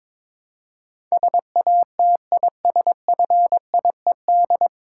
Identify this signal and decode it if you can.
Morse code, heard as a keyed tone, about 21 wpm, 700 Hz. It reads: SATISFIED
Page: Morse Code Ninja practice 22wpm